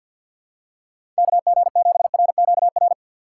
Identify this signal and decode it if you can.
Morse code, heard as a keyed tone, about 51 wpm, 700 Hz. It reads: KG7RQR
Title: USA Call Signs - 4x Character Spacing - 50wpm